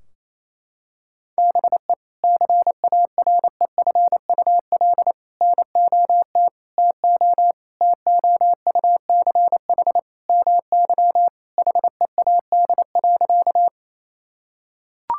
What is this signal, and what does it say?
Morse code, heard as a keyed tone, about 28 wpm, 700 Hz. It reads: BE CAREFUL NOT TO TOUCH MY HEAD.
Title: Sentences from Top 1600 Words 28wpm